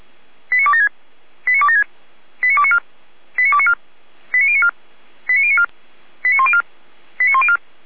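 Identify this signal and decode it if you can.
Five-tone selective call (ZVEI), heard as quick runs of five tones, tones of about 70 ms. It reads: 89276 89276 89294 89294 89094 89094 89104 89104
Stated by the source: ZVEI five-tone call received on an SDR.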